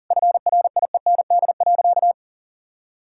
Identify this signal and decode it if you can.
Morse code, heard as a keyed tone, about 40 wpm, 690 Hz. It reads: FRIEND.
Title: Sentences from Top 1000 Words - Review Part 6 - 40wpm